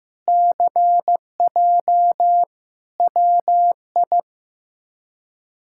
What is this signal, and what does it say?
Morse code, heard as a keyed tone, about 15 wpm, 700 Hz. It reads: CJ WI
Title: K1USN SST QSO Practice - 15wpm